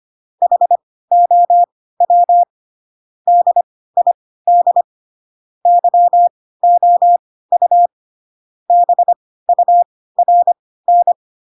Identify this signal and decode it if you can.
Morse code, heard as a keyed tone, about 25 wpm, spaced as for 16 wpm, 700 Hz. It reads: HOW DID YOU BURN
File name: Pinocchio / Pinocchio ch7 10 25 ik1hge